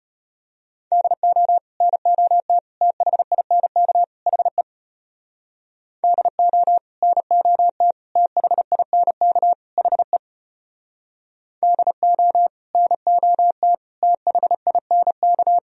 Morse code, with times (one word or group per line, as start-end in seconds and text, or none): 0.9-1.6 DO
1.8-2.6 NOT
2.8-4.0 THINK
4.3-4.6 HE
6.0-6.8 DO
7.0-7.9 NOT
8.2-9.5 THINK
9.8-10.2 HE
11.6-12.5 DO
12.7-13.7 NOT
14.0-15.6 THINK